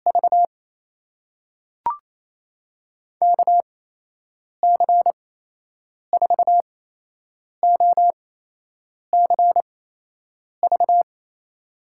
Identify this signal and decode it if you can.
Morse code, heard as a keyed tone, about 28 wpm, spaced as for 8 wpm, 700 Hz. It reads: V KC4OCV
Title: USA Call Signs - 8x Character Spacing - 28wpm